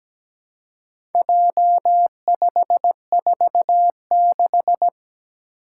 Morse code, heard as a keyed tone, about 17 wpm, 700 Hz. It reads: J546
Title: Four Challenging Letter-Number Combinations 17wpm